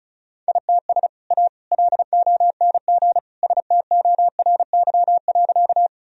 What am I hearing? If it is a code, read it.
Morse code, heard as a keyed tone, about 36 wpm, 700 Hz. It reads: ITS A LONG STORY.
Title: Sentences from Top 1000 Words - Review Part 1 - 35wpm